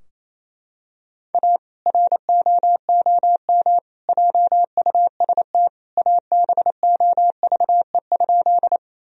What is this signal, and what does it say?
Morse code, heard as a keyed tone, about 28 wpm, 700 Hz. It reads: A ROOM JUST ABOVE?
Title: Sentences from Top 1000 Words - Review Part 2 - 28wpm